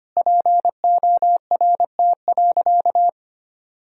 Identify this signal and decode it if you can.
Morse code, heard as a keyed tone, about 25 wpm, 700 Hz. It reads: PORT.
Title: Sentences from Top 1000 Words - Review Part 1 - 25wpm